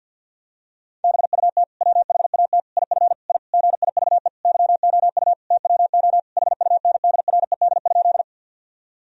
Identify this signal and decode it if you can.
Morse code, heard as a keyed tone, about 51 wpm, 700 Hz. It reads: BUT WHAT IF I GIVE YOU TWO HUNDRED?
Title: Sentences from Top 1000 Words - Review Part 2 - 50wpm